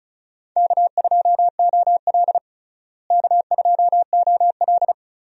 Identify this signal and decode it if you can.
Morse code, heard as a keyed tone, about 34 wpm, 700 Hz. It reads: K2OL K2OL
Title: USA Call Signs - 4 Characters - Repeated 3x - 35wpm